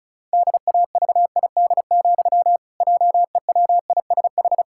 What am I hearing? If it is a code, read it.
Morse code, heard as a keyed tone, about 36 wpm, 700 Hz. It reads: DAVID, JEWISH
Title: Pangram Challenge v2 - 35wpm